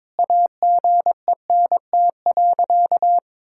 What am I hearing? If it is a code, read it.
Morse code, heard as a keyed tone, about 22 wpm, 700 Hz. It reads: AGENT.
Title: Sentences from Top 1600 Words 22wpm